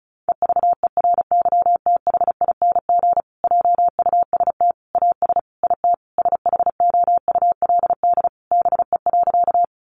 Morse code, heard as a keyed tone, about 34 wpm, 700 Hz. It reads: EVERYTHING JUST AS IT SHOULD BE.
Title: Sentences from Top 700 Words 35wpm